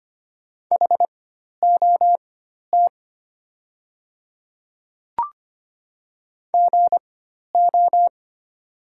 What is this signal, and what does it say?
Morse code, heard as a keyed tone, about 25 wpm, spaced as for 12 wpm, 700 Hz. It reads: HOT GO
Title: 4x Character Spacing - Top 100 Words - 25wpm